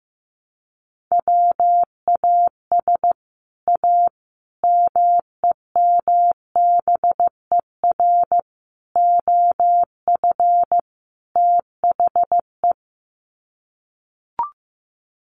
Morse code, heard as a keyed tone, about 15 wpm, 700 Hz. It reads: WAS A MEMBER OF THE
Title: Sets of 5 Words 15wpm